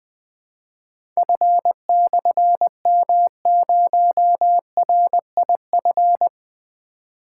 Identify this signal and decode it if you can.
Morse code, heard as a keyed tone, about 20 wpm, 700 Hz. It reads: F/M0RIF
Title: Difficult Call Signs 20wpm